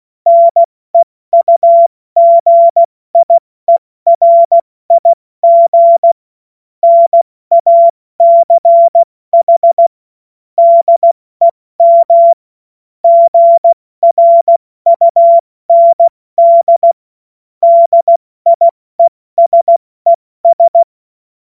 Morse code, heard as a keyed tone, about 16 wpm, 680 Hz. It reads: NEUGIERIG NACH DEM GRUND DIESES